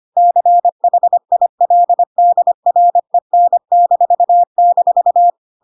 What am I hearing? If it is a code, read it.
Morse code, heard as a keyed tone, about 25 wpm, 700 Hz. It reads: CHILDREN--
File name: Pinocchio / Pinocchio ch3 25 25 ik1hge